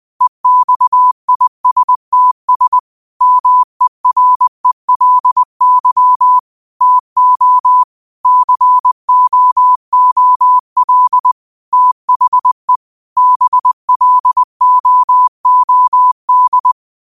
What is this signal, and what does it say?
Morse code, heard as a keyed tone, about 20 wpm, 1000 Hz. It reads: EXISTS MERELY TO COOL THE BLOOD
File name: qotdfiles / 20wpm